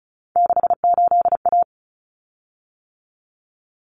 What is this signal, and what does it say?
Morse code, heard as a keyed tone, about 34 wpm, 700 Hz. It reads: 68A
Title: Common Cut Numbers - 3-Digit 35wpm